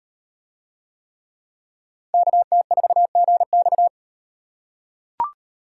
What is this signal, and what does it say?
Morse code, heard as a keyed tone, about 38 wpm, 700 Hz. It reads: KT4GX